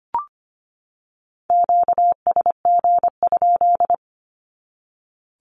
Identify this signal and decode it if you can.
Morse code, heard as a keyed tone, about 25 wpm, 700 Hz. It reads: QSG?
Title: Q-Codes 25wpm